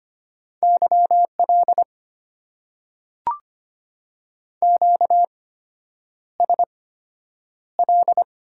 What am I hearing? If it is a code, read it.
Morse code, heard as a keyed tone, about 25 wpm, spaced as for 9 wpm, 700 Hz. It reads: YL QSL
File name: Top 90 QSO Elements - 8x Character Spacing - 25wpm